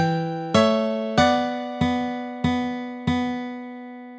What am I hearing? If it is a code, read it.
There is no code.